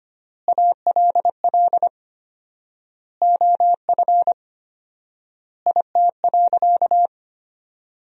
Morse code, heard as a keyed tone, about 25 wpm, 700 Hz. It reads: ALL OF IT.
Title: 4x Word-Spacing - Sentences from Top 500 Words - 25wpm